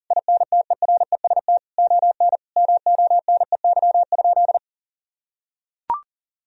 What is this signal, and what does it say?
Morse code, heard as a keyed tone, about 37 wpm, 700 Hz. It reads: INTEREST ON MONEY?
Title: Sentences from Top 1000 Words - Review Part 4 - 40wpm